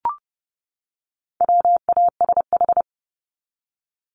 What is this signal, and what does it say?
Morse code, heard as a keyed tone, about 30 wpm, 700 Hz. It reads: WASH
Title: Top 1751-2000 Words 30wpm